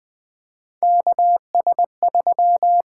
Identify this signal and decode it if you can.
Morse code, heard as a keyed tone, about 20 wpm, 700 Hz. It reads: KS3
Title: USA Call Signs - 4 Characters - 8x Character Spacing - 20wpm